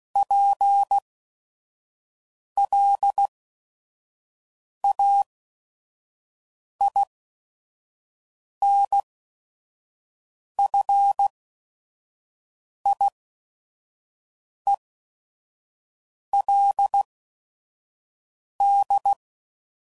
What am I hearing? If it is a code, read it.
Morse code, heard as a keyed tone, about 16 wpm, spaced as for 5 wpm, 800 Hz. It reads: PLAINFIELD
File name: CW Code Groups @5WPM / fromSM's / Cities